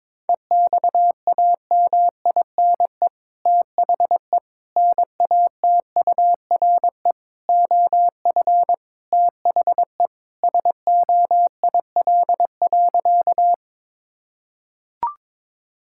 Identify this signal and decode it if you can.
Morse code, heard as a keyed tone, about 22 wpm, 700 Hz. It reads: EXAMINE THE NATURE OF THE SOIL.
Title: Sentences from Top 1600 Words 22wpm